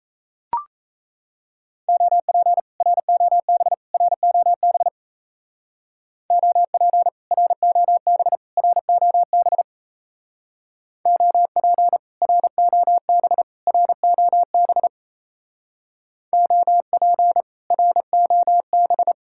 Morse code, heard as a keyed tone, about 35 wpm, 700 Hz. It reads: OP ROB ROB OP ROB ROB OP ROB ROB OP ROB
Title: Speed-Racing Common QSO Phrases Encore - 28wpm